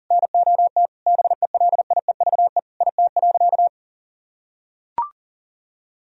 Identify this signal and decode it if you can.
Morse code, heard as a keyed tone, about 40 wpm, 700 Hz. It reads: NOT BELIEVE IT.
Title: Sentences from Top 1000 Words - Review Part 3 - 40wpm